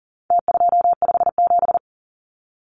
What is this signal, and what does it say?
Morse code, heard as a keyed tone, about 37 wpm, 700 Hz. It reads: T257
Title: Numbers - Contest Style - 4-Digit 40wpm